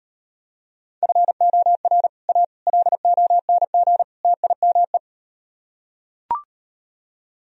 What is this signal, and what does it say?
Morse code, heard as a keyed tone, about 38 wpm, 700 Hz. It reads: FOR A LONG TIME